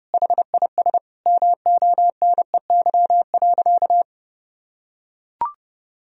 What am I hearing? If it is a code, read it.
Morse code, heard as a keyed tone, about 30 wpm, 700 Hz. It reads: HIS MONEY.
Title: Sentences from Top 1000 Words - Review Part 4 - 30wpm